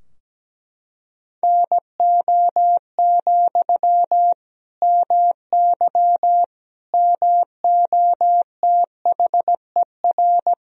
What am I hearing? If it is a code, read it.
Morse code, heard as a keyed tone, about 17 wpm, 700 Hz. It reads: NO, MY MOTHER
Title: Sentences from Top 1000 Words - Review Part 2 - 17wpm